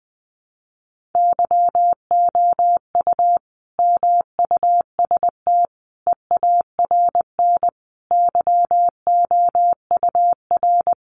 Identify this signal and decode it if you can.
Morse code, heard as a keyed tone, about 20 wpm, 700 Hz. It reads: YOU MUST EARN YOUR
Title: Sentences from Top 1500 Words 20wpm